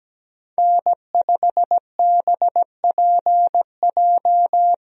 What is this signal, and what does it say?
Morse code, heard as a keyed tone, about 17 wpm, 700 Hz. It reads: N5BPJ
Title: USA Call Signs - 5 Characters - 17wpm